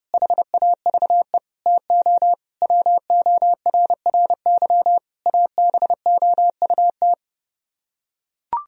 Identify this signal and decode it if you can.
Morse code, heard as a keyed tone, about 32 wpm, 700 Hz. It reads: HAVE TO WORRY ABOUT